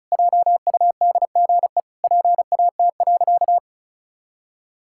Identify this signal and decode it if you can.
Morse code, heard as a keyed tone, about 34 wpm, 700 Hz. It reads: JUDGE PAT.
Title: Pangram Challenge v2 - 35wpm